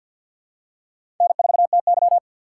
Morse code, heard as a keyed tone, about 48 wpm, 700 Hz. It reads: N4TY
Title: USA Call Signs - 4 Characters - 3x Character Spacing - 50wpm